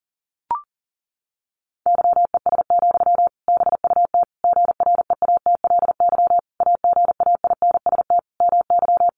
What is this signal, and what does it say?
Morse code, heard as a keyed tone, about 40 wpm, 700 Hz. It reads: YES, BUT GREATLY AGAINST MY
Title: Sentences from Top 1400 Words 40wpm